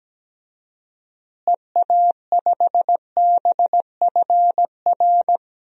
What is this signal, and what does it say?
Morse code, heard as a keyed tone, about 17 wpm, 700 Hz. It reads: EA5BFR
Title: DXCC-100 Call Signs - 6 Characters - Repeated 3x - 17wpm